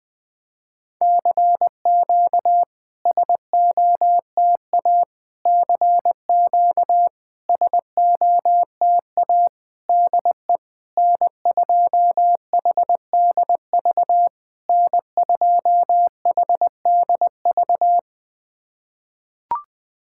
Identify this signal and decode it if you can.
Morse code, heard as a keyed tone, about 20 wpm, 690 Hz. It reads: CQ SOTA CQ SOTA DE N2HDV N2HDV
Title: SOTA QSO Practice 20wpm